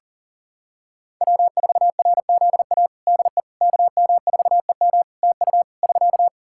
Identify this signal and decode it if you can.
Morse code, heard as a keyed tone, about 40 wpm, 700 Hz. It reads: W4RZA DE KM4EM TU <SK>
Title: Practice QSO - Exchange 4 - Ending - 40wpm